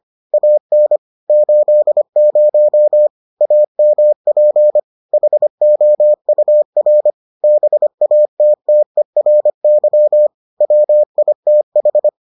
Morse code, heard as a keyed tone, about 25 wpm, 600 Hz. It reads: AN 80 AMP HOUR BATTERY WITH